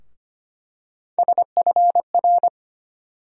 Morse code, heard as a keyed tone, about 25 wpm, 700 Hz. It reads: SFR
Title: USA Call Sign Postfixes - 3 Letters 25wpm